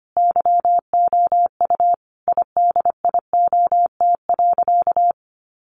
Morse code, heard as a keyed tone, about 25 wpm, 700 Hz. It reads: YOU IDIOT.
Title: A Sound Of Thunder 25wpm